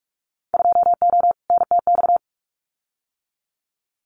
Morse code, heard as a keyed tone, about 46 wpm, 700 Hz. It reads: 2O NTX
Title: Winter Field Day Exchanges 45wpm